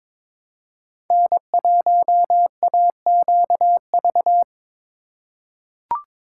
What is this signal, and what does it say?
Morse code, heard as a keyed tone, about 22 wpm, 700 Hz. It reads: N1AQV
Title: USA Call Signs - 5 Characters - 2x Character Spacing - 22wpm